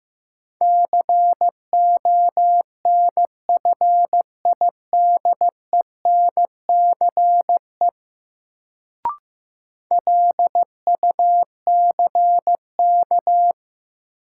Morse code, heard as a keyed tone, about 15 wpm, 700 Hz. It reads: CONFIDENCE LUCK
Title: Sentences from Top 1500 Words - New Word Drill - 15wpm